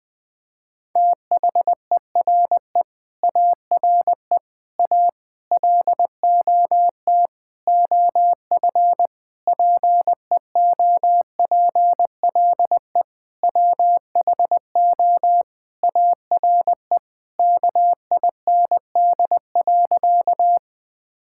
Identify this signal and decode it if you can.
Morse code, heard as a keyed tone, about 19 wpm, 700 Hz. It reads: THERE ARE A LOT OF PEOPLE WHO ARE KIND.
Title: Sentences with 8-Word Phrases - Spoken Beginning - 20wpm